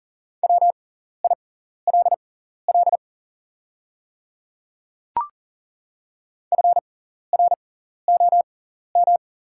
Morse code, heard as a keyed tone, about 39 wpm, spaced as for 14 wpm, 690 Hz. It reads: WILL FROM
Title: 6x Character Spacing - Top 100 Words - 40wpm